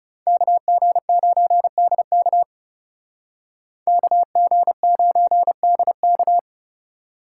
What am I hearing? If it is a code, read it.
Morse code, heard as a keyed tone, about 35 wpm, 700 Hz. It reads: KG9DK KG9DK
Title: Speed Racing - USA 2x2 Call Signs 30wpm